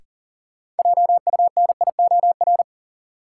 Morse code, heard as a keyed tone, about 40 wpm, 700 Hz. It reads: JUNIOR
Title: Sentences from Top 1500 Words - New Word Drill Alt - 40wpm